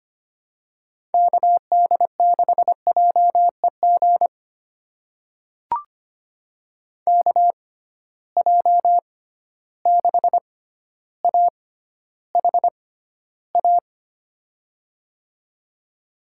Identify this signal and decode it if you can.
Morse code, heard as a keyed tone, about 25 wpm, spaced as for 13 wpm, 700 Hz. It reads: KD6JEG KJ6AHA
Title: USA Call Signs - 6 Characters - 6x Character Spacing - 25wpm